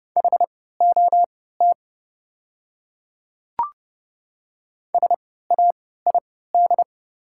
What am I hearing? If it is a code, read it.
Morse code, heard as a keyed tone, about 30 wpm, spaced as for 17 wpm, 700 Hz. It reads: HOT SAID
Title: 3x Character Spacing - Top 100 Words - 30wpm